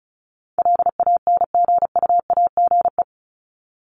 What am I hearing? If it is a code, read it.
Morse code, heard as a keyed tone, about 34 wpm, 700 Hz. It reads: LANGUAGE
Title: Top 500 Words 35wpm